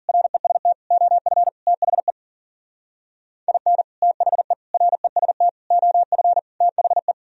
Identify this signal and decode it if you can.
Morse code, heard as a keyed tone, about 42 wpm, 700 Hz. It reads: REST OF THE IN THE REST OF THE
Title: Speed-Racing - Sets of 5 Words 40wpm